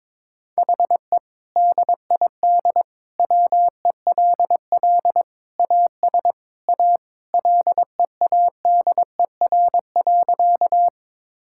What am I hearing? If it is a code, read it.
Morse code, heard as a keyed tone, about 22 wpm, 700 Hz. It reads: HE DID WELL AS A LEADER.
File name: practice / 22wpm / Sentences with 2-Word Phrases - 22wpm